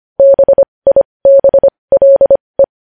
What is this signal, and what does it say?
Morse code, heard as a keyed tone, about 25 wpm, spaced as for 20 wpm, 550 Hz. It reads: BIBLE
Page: news-headlines-in-morse-code-at-20-wpm